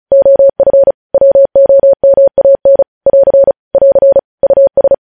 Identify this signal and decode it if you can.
Morse code, heard as a keyed tone, about 36 wpm, 550 Hz. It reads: OF WOMAN <AR> <AR> US
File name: podcast / News headlines-1x-35x35